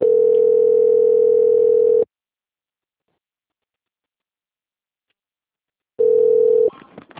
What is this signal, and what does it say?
Ringback tone (North America), 440 + 480 Hz.